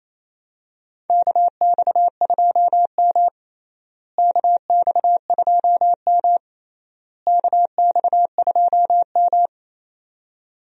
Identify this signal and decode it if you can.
Morse code, heard as a keyed tone, about 28 wpm, 700 Hz. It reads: KX2M KX2M KX2M
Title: USA Call Signs - 4 Characters - Repeated 3x - 28wpm